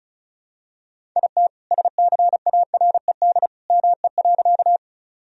Morse code, heard as a keyed tone, about 36 wpm, 700 Hz. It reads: IT SCARED ME.
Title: Sentences from Top 1000 Words - Review Part 4 - 35wpm